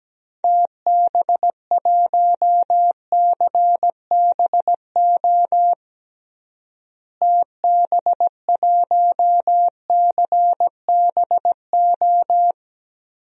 Morse code, heard as a keyed tone, about 17 wpm, 700 Hz. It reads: TB1CBO TB1CBO
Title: DXCC-100 Call Signs - 6 Characters - Repeated 3x - 17wpm